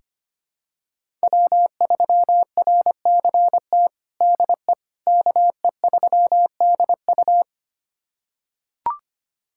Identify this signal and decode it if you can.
Morse code, heard as a keyed tone, about 25 wpm, 700 Hz. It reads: W3RCT DE KE3DU
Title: USA Callsign DE Callsign - 5 Characters - 25wpm